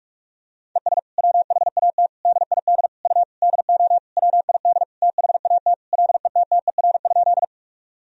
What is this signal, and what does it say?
Morse code, heard as a keyed tone, about 46 wpm, 700 Hz. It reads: ES WHAT DID U DO WID THAT LETTER?
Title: Sentences from Top 300 Words and Abbreviations 45wpm